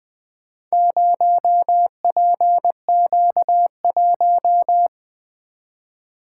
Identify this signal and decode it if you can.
Morse code, heard as a keyed tone, about 20 wpm, 700 Hz. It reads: 0PQ1
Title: Four Letter-Number Combinations - 20wpm